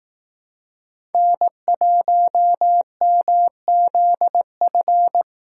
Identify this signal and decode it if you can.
Morse code, heard as a keyed tone, about 18 wpm, 700 Hz. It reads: N1MZF
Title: Speed Racing - USA 1x3 Call Signs 15wpm